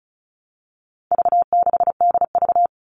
Morse code, heard as a keyed tone, about 34 wpm, 700 Hz. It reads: V6DV
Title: All Cut Numbers - 4-Digit 35wpm